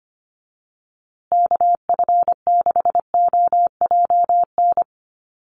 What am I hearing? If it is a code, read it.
Morse code, heard as a keyed tone, about 25 wpm, 700 Hz. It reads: KF6OJN
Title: USA Call Signs Repeated 2x 25wpm